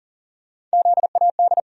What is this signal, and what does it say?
Morse code, heard as a keyed tone, about 40 wpm, 700 Hz. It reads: ZAD
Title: USA Call Sign Postfixes - 3 Letters 40wpm